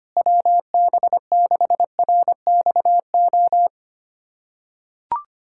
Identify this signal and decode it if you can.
Morse code, heard as a keyed tone, about 25 wpm, 700 Hz. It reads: WB6RXO
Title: USA Callsign DE Callsign - 6 Characters - 25wpm